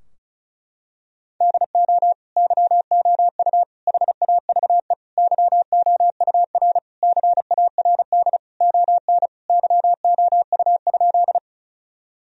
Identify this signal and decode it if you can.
Morse code, heard as a keyed tone, about 34 wpm, 700 Hz. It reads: DO YOU HAVE YOUR CARD ON YOU?
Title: Sentences from Top 1000 Words - Review Part 4 - 35wpm